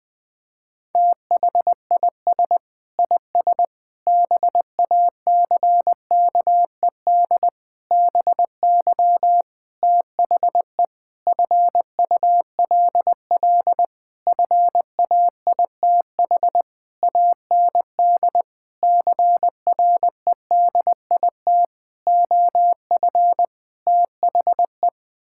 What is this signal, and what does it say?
Morse code, heard as a keyed tone, about 20 wpm, 700 Hz. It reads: THIS IS BACKED BY THE FULL FAITH AND CREDIT OF THE
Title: Sentences with 9-Word Phrases - Spoken Beginning - 20wpm